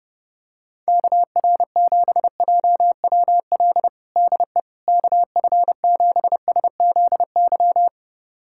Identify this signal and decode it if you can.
Morse code, heard as a keyed tone, about 30 wpm, 700 Hz. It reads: KR7JWL DE KF7SZY
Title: USA Callsign DE Callsign - 6 Characters - 30wpm